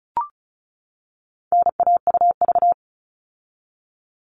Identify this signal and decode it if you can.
Morse code, heard as a keyed tone, about 34 wpm, 700 Hz. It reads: NAUV
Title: All Cut Numbers - 4-Digit 35wpm